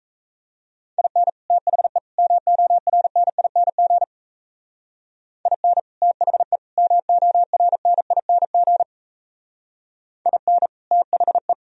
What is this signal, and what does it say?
Morse code, heard as a keyed tone, about 33 wpm, 700 Hz. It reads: IN THE MORNING IN THE MORNING IN THE
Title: Speed-Racing - Sets of 3 Words 28wpm